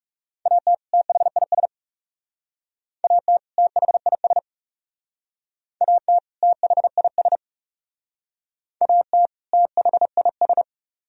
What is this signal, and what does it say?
Morse code, heard as a keyed tone, about 37 wpm, 700 Hz. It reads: AT THIS AT THIS AT THIS AT THIS
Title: Speed-Racing - Sets of 2 Words 30wpm